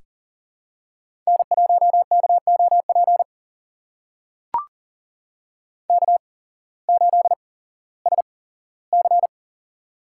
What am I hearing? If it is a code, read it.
Morse code, heard as a keyed tone, about 40 wpm, spaced as for 19 wpm, 700 Hz. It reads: N1KOP K8SC
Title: USA Call Signs - 8x Character Spacing - 40wpm